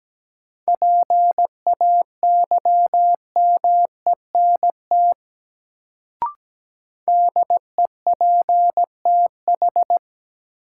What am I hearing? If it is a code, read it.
Morse code, heard as a keyed tone, about 17 wpm, 700 Hz. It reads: PAYMENT DEPTH
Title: Sentences from Top 1200 Words - New Word Drill - 17wpm